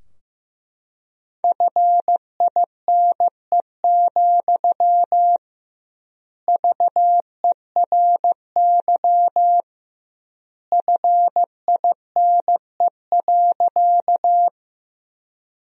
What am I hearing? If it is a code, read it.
Morse code, heard as a keyed tone, about 15 wpm, 700 Hz. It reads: FINE, VERY FINE.
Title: 2x Word-Spacing - Sentences from Top 500 Words - 15wpm